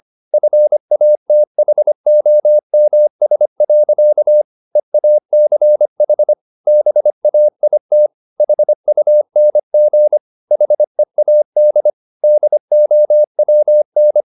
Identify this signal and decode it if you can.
Morse code, heard as a keyed tone, about 25 wpm, 600 Hz. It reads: FATHOMS. EACH BAIT HUNG HEAD DOWN
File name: the old man and the sea part 1 25wpm